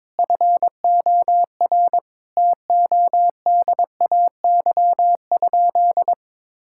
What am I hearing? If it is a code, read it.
Morse code, heard as a keyed tone, about 22 wpm, 700 Hz. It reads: FOR TODAY?
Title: Sentences from Top 1000 Words - Review Part 5 - 22wpm